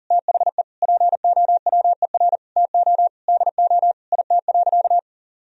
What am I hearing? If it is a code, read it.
Morse code, heard as a keyed tone, about 40 wpm, 700 Hz. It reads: THE POWER TO DO IT.
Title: Sentences from Top 1000 Words - Review Part 3 - 40wpm